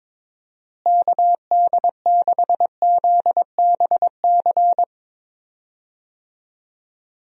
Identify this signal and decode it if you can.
Morse code, heard as a keyed tone, about 22 wpm, 700 Hz. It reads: KD6ZBC
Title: USA Call Signs - 6 Characters - Repeated 3x - 22wpm